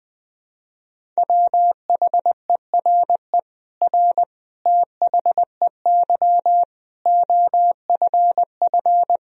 Morse code, heard as a keyed tone, about 20 wpm, 700 Hz. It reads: WHERE R THEY OFF
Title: Sentences from Top 200 Words and Abbreviations 20wpm